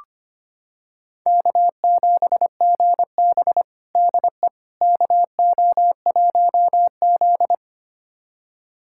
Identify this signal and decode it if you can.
Morse code, heard as a keyed tone, about 25 wpm, 700 Hz. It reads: K7GB DE KO1Z
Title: USA Callsign DE Callsign - 4 Characters - 25wpm